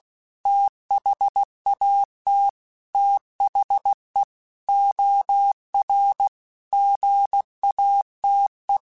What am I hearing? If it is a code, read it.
Morse code, heard as a keyed tone, about 16 wpm, 800 Hz. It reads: THAT THE OR GATE
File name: Technical Subjects / Technical Subjects-mp3s / OR Gate @15